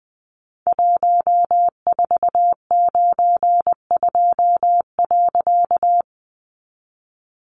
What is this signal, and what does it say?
Morse code, heard as a keyed tone, about 20 wpm, 700 Hz. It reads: 1492.
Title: A Sound Of Thunder 20wpm